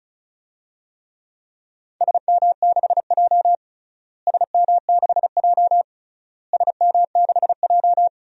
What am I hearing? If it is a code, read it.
Morse code, heard as a keyed tone, about 36 wpm, 700 Hz. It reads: SM6J SM6J SM6J
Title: DXCC-100 Call Signs - 4 Characters - Repeated 3x - 35wpm